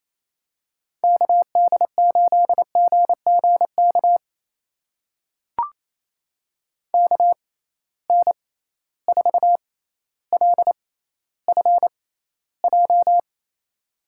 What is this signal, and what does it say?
Morse code, heard as a keyed tone, about 28 wpm, spaced as for 14 wpm, 700 Hz. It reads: KD8GGK KN4LFJ
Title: USA Call Signs - 6 Characters - 6x Character Spacing - 28wpm